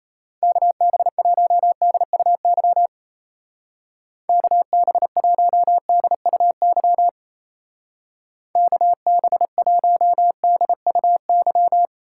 Morse code, heard as a keyed tone, about 35 wpm, 700 Hz. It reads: KB1DUY KB1DUY KB1DUY